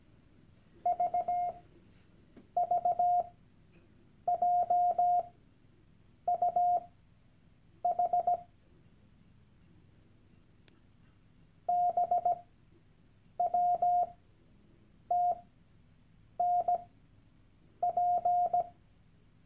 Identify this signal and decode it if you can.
Morse code, heard as a keyed tone, about 17 wpm, spaced as for 7 wpm, 710 Hz. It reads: VVJUH BWTNP
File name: Workshop Dateien / CW-Online-Kurs / Lektion 001